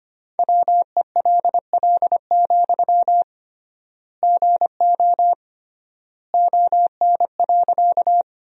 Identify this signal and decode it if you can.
Morse code, heard as a keyed tone, about 25 wpm, 700 Hz. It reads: WELL, GO ON.
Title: 3x Word-Spacing - Sentences from Top 200 Words - 25wpm